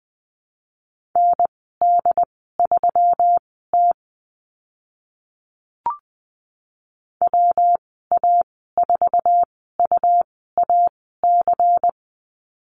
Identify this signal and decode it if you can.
Morse code, heard as a keyed tone, about 20 wpm, spaced as for 15 wpm, 700 Hz. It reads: ND3T WA4UAC
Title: 122 Lesson 20g - Number 3 - Call Signs 20wpm